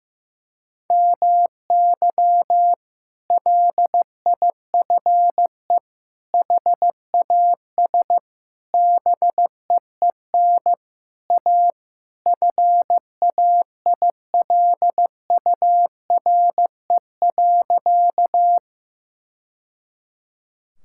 Morse code, heard as a keyed tone, about 15 wpm, 700 Hz. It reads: MY LIFE HAS BEEN A FAILURE.
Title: Sentences from Top 1000 Words - Review Part 6 - 15wpm